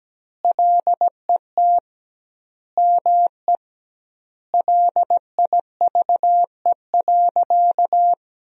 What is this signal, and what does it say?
Morse code, heard as a keyed tone, about 17 wpm, 700 Hz. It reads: LET ME LIVE.
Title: 2x Word-Spacing - Sentences from Top 300 Words - 17wpm